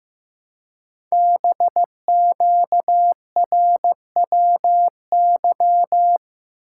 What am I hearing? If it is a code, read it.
Morse code, heard as a keyed tone, about 15 wpm, 700 Hz. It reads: BQRWY
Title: Code Groups - 5 Letters - 15wpm